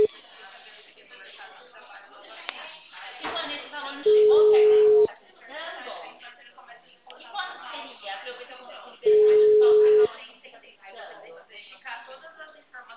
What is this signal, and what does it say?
Ringback tone (Europe, 425 Hz), with speech.
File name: out- -2013- - - .687